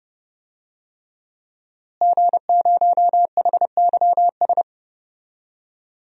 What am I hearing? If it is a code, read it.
Morse code, heard as a keyed tone, about 32 wpm, 700 Hz. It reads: G0HYS
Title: DXCC-100 Call Signs - 5 Characters - Repeated 3x - 30wpm